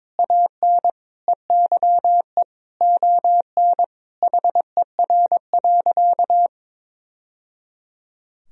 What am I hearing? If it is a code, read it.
Morse code, heard as a keyed tone, about 22 wpm, 700 Hz. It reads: AN EYE ON HER.
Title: Sentences from Top 1000 Words - Review Part 1 - 22wpm